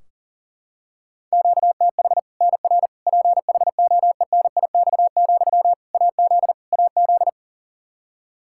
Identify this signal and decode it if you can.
Morse code, heard as a keyed tone, about 40 wpm, 700 Hz. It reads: QTH NR PHOENIX, AZ AZ
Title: Practice QSO - Exchange 1 - Beginning - 40wpm